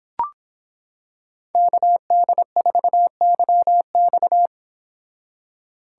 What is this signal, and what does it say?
Morse code, heard as a keyed tone, about 26 wpm, 700 Hz. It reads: KD4YX